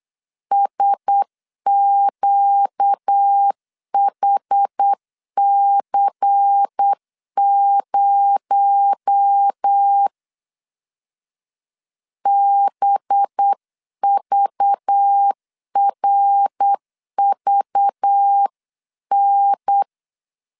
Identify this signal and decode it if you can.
Morse code, heard as a keyed tone, about 8 wpm, 800 Hz. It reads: SQHC0 BVRVN
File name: 8wpm